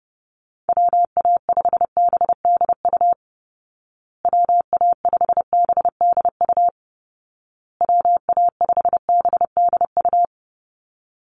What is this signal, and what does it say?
Morse code, heard as a keyed tone, about 30 wpm, 700 Hz. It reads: WA5BDU WA5BDU WA5BDU
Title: Call Signs Repeated 2x - 30wpm